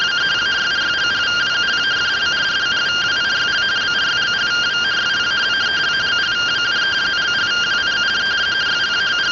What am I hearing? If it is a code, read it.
Two-tone teleprinter signal, about 170 Hz shift.